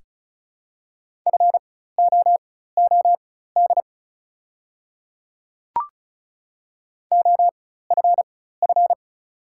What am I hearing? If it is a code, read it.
Morse code, heard as a keyed tone, about 34 wpm, spaced as for 16 wpm, 690 Hz. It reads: FOOD OFF
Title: 4x Character Spacing - Top 500 Words - 35wpm